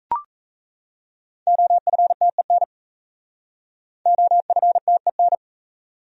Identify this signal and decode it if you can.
Morse code, heard as a keyed tone, about 37 wpm, 700 Hz. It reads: OFTEN OFTEN